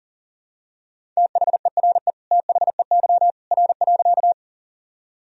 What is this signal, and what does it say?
Morse code, heard as a keyed tone, about 37 wpm, 700 Hz. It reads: THERE THEY R.
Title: Sentences from Top 100 Words and Abbreviations 40wpm